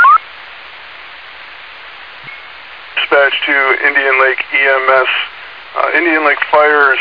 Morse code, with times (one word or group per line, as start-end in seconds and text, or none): none